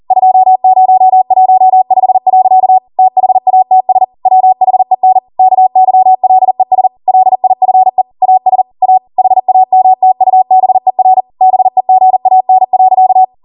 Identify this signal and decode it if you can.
Morse code, heard as a keyed tone, about 40 wpm, 750 Hz. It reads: 2015. THATS WHEN KYLES LIFE AS A HAMTUBER BEGAN.